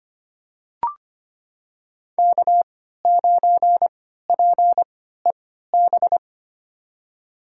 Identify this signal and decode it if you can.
Morse code, heard as a keyed tone, about 25 wpm, spaced as for 14 wpm, 700 Hz. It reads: K9PEB
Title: USA Call Signs - 5 Characters - 3x Character Spacing - 25wpm